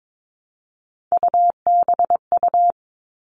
Morse code, heard as a keyed tone, about 22 wpm, 700 Hz. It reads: UBU
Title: All Cut Numbers - 3-Digit 22wpm